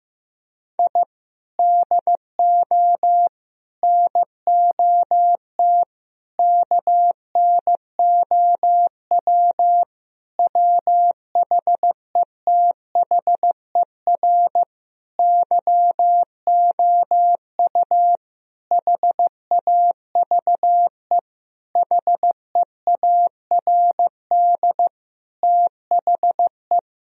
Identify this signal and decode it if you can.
Morse code, heard as a keyed tone, about 15 wpm, 700 Hz. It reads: I DO NOT KNOW WHETHER YOU HAVE HEARD THE
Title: Sentences from Top 1000 Words - Review Part 3 - 15wpm